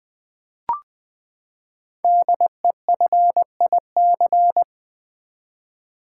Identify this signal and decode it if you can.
Morse code, heard as a keyed tone, about 20 wpm, 700 Hz. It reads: DEFIC
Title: Build A Word - 5 Letters At A Time - 10-Letter Words - 20wpm